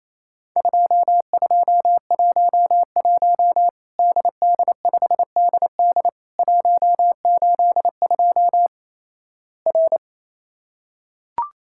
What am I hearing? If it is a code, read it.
Morse code, heard as a keyed tone, about 28 wpm, 700 Hz. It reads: 2211 DD5DD 182 R
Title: Worked All Europe DX Contest - QTC Exchange - 28wpm